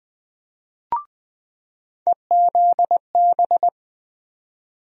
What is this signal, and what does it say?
Morse code, heard as a keyed tone, about 21 wpm, 700 Hz. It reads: EZB
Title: USA Call Sign Postfixes - 3 Letters 20wpm